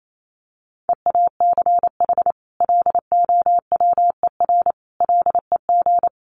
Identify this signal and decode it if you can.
Morse code, heard as a keyed tone, about 28 wpm, 700 Hz. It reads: EACH LOWER LEG
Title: A Sound Of Thunder 28wpm